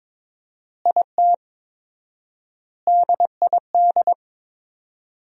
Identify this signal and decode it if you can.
Morse code, heard as a keyed tone, about 21 wpm, 700 Hz. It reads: IT DID
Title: 4x Word-Spacing - Sentences from Top 200 Words - 22wpm